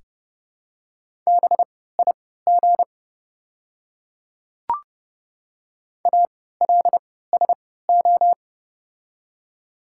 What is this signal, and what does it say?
Morse code, heard as a keyed tone, about 31 wpm, spaced as for 17 wpm, 690 Hz. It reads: BIG ALSO
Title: 3x Character Spacing - Top 200 Words - 30wpm